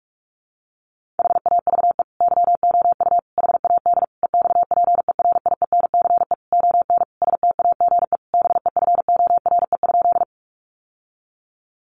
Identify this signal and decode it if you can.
Morse code, heard as a keyed tone, about 44 wpm, 700 Hz. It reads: HAVE YOU HAD EXPERIENCE ON STAGE BEFORE?
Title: Sentences from Top 1200 Words 45wpm